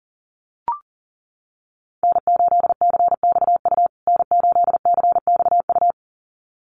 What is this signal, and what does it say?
Morse code, heard as a keyed tone, about 40 wpm, 700 Hz. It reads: N8CXU N8CXU
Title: USA Call Signs Repeated 2x 40wpm